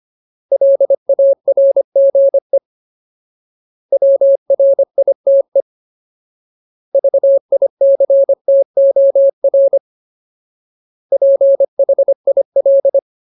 Morse code, heard as a keyed tone, about 25 wpm, 550 Hz. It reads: LARGE WRITE VICTOR PHIL
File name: Ultimate Word List / 25WPM Ultimate